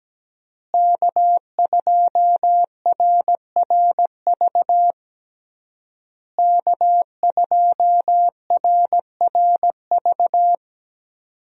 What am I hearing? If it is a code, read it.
Morse code, heard as a keyed tone, about 17 wpm, 700 Hz. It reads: K2RRV K2RRV
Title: USA Call Signs - 5 Characters - Repeated 3x - 17wpm